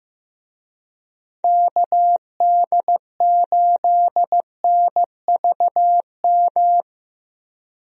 Morse code, heard as a keyed tone, about 15 wpm, 700 Hz. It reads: KD8NVM
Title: USA Call Signs - 6 Characters - Repeated 2x - 15wpm